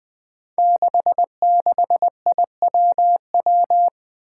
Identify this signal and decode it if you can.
Morse code, heard as a keyed tone, about 20 wpm, 700 Hz. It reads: 66IWW
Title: Five Challenging Letter-Number Combinations 20wpm